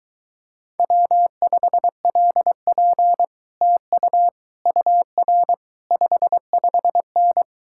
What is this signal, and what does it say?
Morse code, heard as a keyed tone, about 24 wpm, 700 Hz. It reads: W5LP TU UR 55N